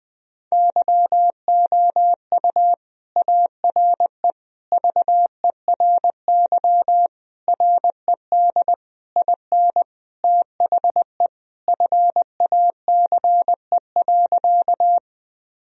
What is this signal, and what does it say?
Morse code, heard as a keyed tone, about 20 wpm, 700 Hz. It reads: YOU ARE VERY RED IN THE FACE.
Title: Sentences from Top 1000 Words - Review Part 2 - 20wpm